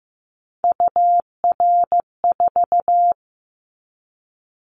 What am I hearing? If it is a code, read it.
Morse code, heard as a keyed tone, about 15 wpm, 700 Hz. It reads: UR4
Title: Callsign Prefixes for Top 25 DXCC - 15wpm